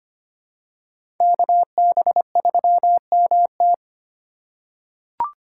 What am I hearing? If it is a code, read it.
Morse code, heard as a keyed tone, about 25 wpm, 700 Hz. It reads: KB3MT
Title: USA Call Signs - 3x Character Spacing - 25wpm